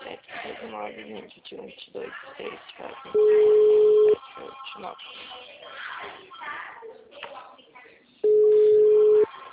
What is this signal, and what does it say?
Ringback tone (Europe, 425 Hz), with speech.
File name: out- -2055- - - .55035